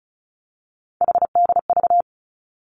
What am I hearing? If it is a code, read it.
Morse code, heard as a keyed tone, about 34 wpm, 700 Hz. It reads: HDV
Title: Three Challenging Letter-Number Combinations 35wpm